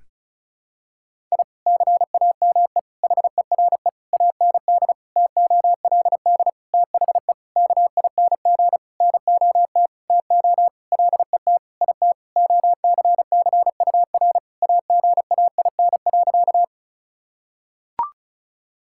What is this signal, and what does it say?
Morse code, heard as a keyed tone, about 36 wpm, 700 Hz. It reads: I CAME HERE AND TOLD THE KING NOT TO LET IT OCCUR AGAIN.
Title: Sentences from Top 1000 Words - Review Part 6 - 35wpm